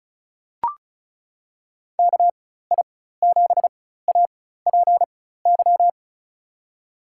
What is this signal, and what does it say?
Morse code, heard as a keyed tone, about 36 wpm, spaced as for 16 wpm, 700 Hz. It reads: KI7APY
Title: USA Call Signs - 6 Characters - 4x Character Spacing - 35wpm